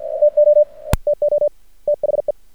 Morse code, heard as a keyed tone, about 43 wpm, 600 Hz. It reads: TO TO THE
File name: archived / CRSnet